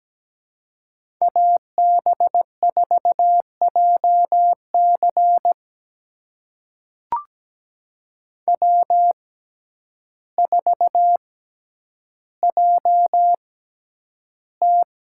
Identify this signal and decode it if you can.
Morse code, heard as a keyed tone, about 17 wpm, spaced as for 9 wpm, 700 Hz. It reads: AB4JC W4JT
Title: USA Call Signs - 5 Characters - 6x Character Spacing - 17wpm